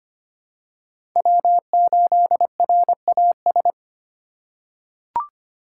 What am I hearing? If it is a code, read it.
Morse code, heard as a keyed tone, about 25 wpm, 700 Hz. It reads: W8RAS
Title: USA Call Signs - 5 Characters - 3x Character Spacing - 25wpm